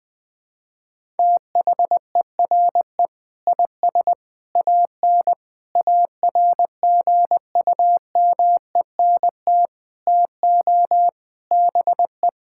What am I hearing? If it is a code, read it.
Morse code, heard as a keyed tone, about 20 wpm, 700 Hz. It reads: THERE IS AN ARGUMENT TO BE
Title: Sentences with 8-Word Phrases - 20wpm